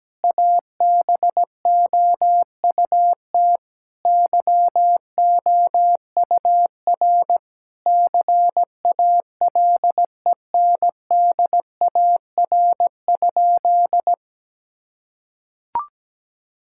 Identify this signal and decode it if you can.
Morse code, heard as a keyed tone, about 17 wpm, 700 Hz. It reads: ABOUT YOUR CALENDAR?
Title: Sentences from Top 1600 Words 17wpm